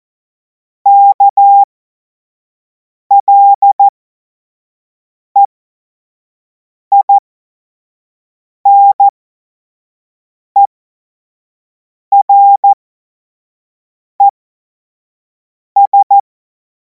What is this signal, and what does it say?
Morse code, heard as a keyed tone, about 14 wpm, spaced as for 5 wpm, 800 Hz. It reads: KLEINERES
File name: Morsekurs / Witze / Witz 12. Mai 2026 25 F70